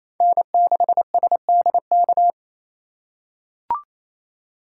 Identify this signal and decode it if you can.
Morse code, heard as a keyed tone, about 27 wpm, 700 Hz. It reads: N6SDK
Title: USA Call Signs - 2x Character Spacing - 28wpm